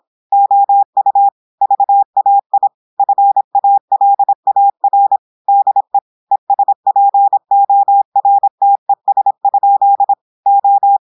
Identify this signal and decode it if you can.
Morse code, heard as a keyed tone, about 26 wpm, 800 Hz. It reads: OU VAI FALAR DE ESPORTES? O
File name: portugais texte en clair 2 vitesse 26 mots minutes